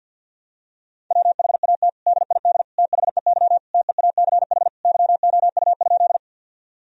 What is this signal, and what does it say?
Morse code, heard as a keyed tone, about 50 wpm, 700 Hz. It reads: WHAT DID THEY TEACH YOU?
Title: Sentences from Top 1000 Words - Review Part 2 - 50wpm